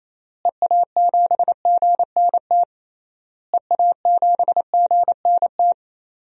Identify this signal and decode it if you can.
Morse code, heard as a keyed tone, about 28 wpm, 700 Hz. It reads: EA7GNT EA7GNT
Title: DXCC-100 Call Signs - 6 Characters - Repeated 3x - 28wpm